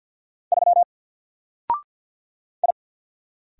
Morse code, heard as a keyed tone, about 50 wpm, 700 Hz. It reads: 3 I
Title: Single Letter-Number 50wpm